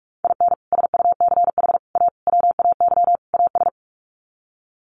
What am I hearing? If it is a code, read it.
Morse code, heard as a keyed tone, about 44 wpm, 700 Hz. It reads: IN SUCH A WAY AS
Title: Sets of 5 Words Encore 45wpm